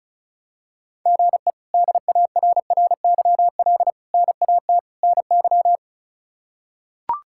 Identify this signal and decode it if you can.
Morse code, heard as a keyed tone, about 34 wpm, 700 Hz. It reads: GE DARRYL NAT NY
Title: K1USN SST QSO Practice - 35wpm